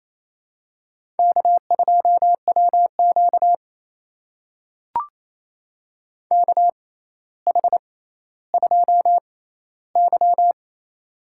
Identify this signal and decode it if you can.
Morse code, heard as a keyed tone, about 28 wpm, spaced as for 14 wpm, 700 Hz. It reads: K2WQ KH2Y
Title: USA Call Signs - 4 Characters - 6x Character Spacing - 28wpm